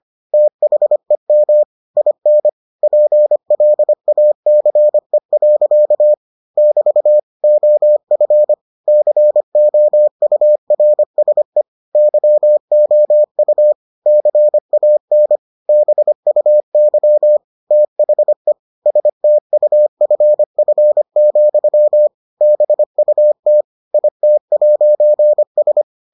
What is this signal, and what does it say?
Morse code, heard as a keyed tone, about 25 wpm, 600 Hz. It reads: THEM IN PLACE. <BT> OF COURSE YOU CAN BUY THE STUFF, BUT IT'S